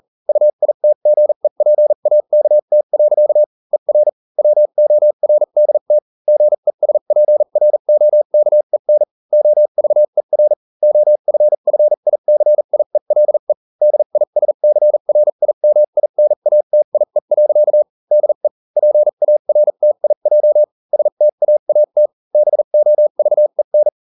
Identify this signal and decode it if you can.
Morse code, heard as a keyed tone, about 40 wpm, spaced as for 35 wpm, 600 Hz. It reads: UITGEPAKT. ER WORDT GESPROKEN OVER OFFICIELE DISCRIMINATIE. DE PARTIJ STAAT BOVEN